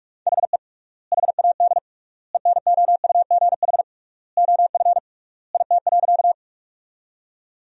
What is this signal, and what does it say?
Morse code, heard as a keyed tone, about 44 wpm, 700 Hz. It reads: HE HAD ENOUGH OF IT.
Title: 3x Word-Spacing - Sentences from Top 400 Words - 45wpm